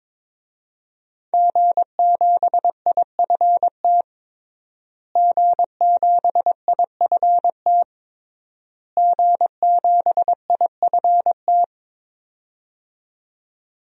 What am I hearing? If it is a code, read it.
Morse code, heard as a keyed tone, about 21 wpm, 700 Hz. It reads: G7IFT G7IFT G7IFT
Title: DXCC-100 Call Signs - 5 Characters - Repeated 3x - 22wpm